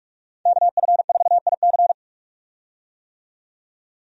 Morse code, heard as a keyed tone, about 46 wpm, 700 Hz. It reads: KF4IC
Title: USA Call Signs - 5 Characters - 45wpm